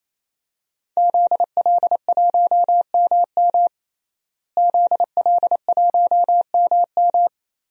Morse code, heard as a keyed tone, about 28 wpm, 700 Hz. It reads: ZL1MM ZL1MM
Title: DXCC-100 Call Signs - 5 Characters - Repeated 3x - 28wpm